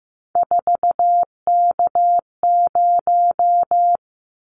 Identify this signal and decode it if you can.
Morse code, heard as a keyed tone, about 15 wpm, 700 Hz. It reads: 4K0
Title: Callsign Prefixes for Top 100 DXCC - 15wpm